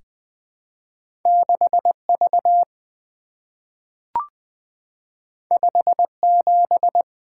Morse code, heard as a keyed tone, about 20 wpm, 700 Hz. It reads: 6V 57
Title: Two Letter-Number Combinations - 20wpm